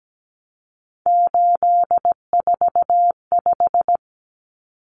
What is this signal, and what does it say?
Morse code, heard as a keyed tone, about 17 wpm, 700 Hz. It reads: 845
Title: Common Cut Numbers - 3-Digit 17wpm